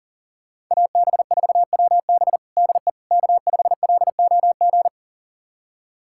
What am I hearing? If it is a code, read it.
Morse code, heard as a keyed tone, about 40 wpm, 700 Hz. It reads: AB4WB DE K5LOG
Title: USA Callsign DE Callsign - 5 Characters - 40wpm